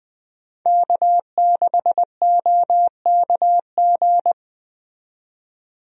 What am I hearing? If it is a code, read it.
Morse code, heard as a keyed tone, about 20 wpm, 700 Hz. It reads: K6OKG
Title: USA Call Signs - 5 Characters - 20wpm